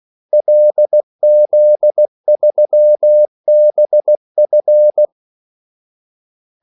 Morse code, heard as a keyed tone, about 16 wpm, 600 Hz. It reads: LZ3BF
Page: Kaldesignaler 16 wpm